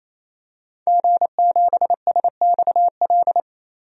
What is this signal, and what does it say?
Morse code, heard as a keyed tone, about 28 wpm, 700 Hz. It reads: G7SXL
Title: DXCC-100 Call Signs - 5 Characters - 28wpm